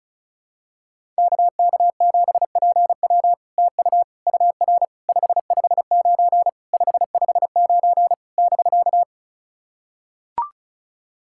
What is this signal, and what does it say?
Morse code, heard as a keyed tone, about 36 wpm, 700 Hz. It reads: KK7PW TU UR 559 559 <BK>
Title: SOTA QSO Practice 35wpm